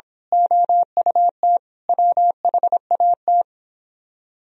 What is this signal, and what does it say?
Morse code, heard as a keyed tone, about 26 wpm, 700 Hz. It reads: OUT WHAT
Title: Speed-Racing - Sets of 5 Words 22wpm